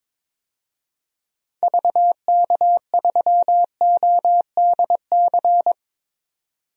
Morse code, heard as a keyed tone, about 22 wpm, 700 Hz. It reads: VK3ODC
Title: DXCC-100 Call Signs - 6 Characters - 22wpm